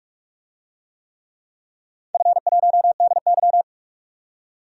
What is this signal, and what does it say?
Morse code, heard as a keyed tone, about 42 wpm, 700 Hz. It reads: F1DY
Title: DXCC-100 Call Signs - 4 Characters - 45wpm